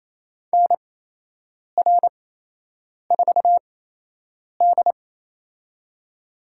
Morse code, heard as a keyed tone, about 28 wpm, spaced as for 8 wpm, 700 Hz. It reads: NR4D
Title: USA Call Signs - 4 Characters - 8x Character Spacing - 28wpm